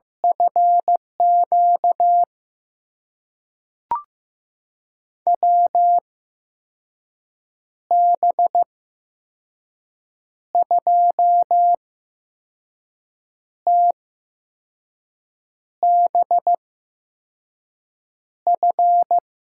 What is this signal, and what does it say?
Morse code, heard as a keyed tone, about 16 wpm, spaced as for 5 wpm, 700 Hz. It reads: FQ WB2TBF
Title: USA Call Signs - 8x Character Spacing - 15wpm